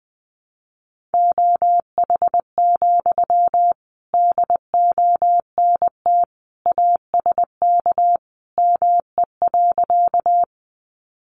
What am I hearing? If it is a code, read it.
Morse code, heard as a keyed tone, about 20 wpm, 700 Hz. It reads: OH, DONT ASK ME.
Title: Sentences from Top 500 Words 20wpm